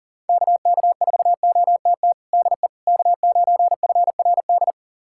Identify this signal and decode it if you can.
Morse code, heard as a keyed tone, about 40 wpm, 700 Hz. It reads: KK4OTT DE K9FRD
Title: USA Callsign DE Callsign 40wpm